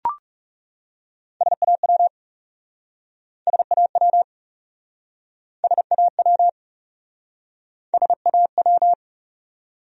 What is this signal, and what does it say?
Morse code, heard as a keyed tone, about 37 wpm, 700 Hz. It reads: SAW SAW SAW SAW
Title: Speed-Racing Top 300 Words 30wpm